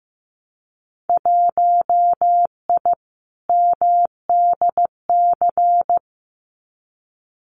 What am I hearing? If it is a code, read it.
Morse code, heard as a keyed tone, about 15 wpm, 700 Hz. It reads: 1I MDC
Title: Winter Field Day Exchanges 15wpm